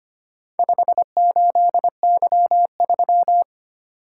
Morse code, heard as a keyed tone, about 25 wpm, 700 Hz. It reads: 58Y3
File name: practice / 25wpm / Four Letter-Number Combinations - 25wpm